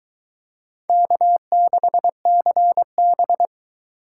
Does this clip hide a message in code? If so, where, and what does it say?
Yes: Morse code from 0.9 to 3.5 s, heard as a keyed tone, about 23 wpm, 700 Hz. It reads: K6CB